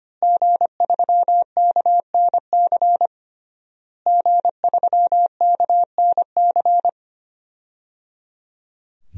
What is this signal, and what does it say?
Morse code, heard as a keyed tone, about 25 wpm, 700 Hz. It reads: G3KNC G3KNC
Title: DXCC-100 Call Signs - 5 Characters - Repeated 3x - 25wpm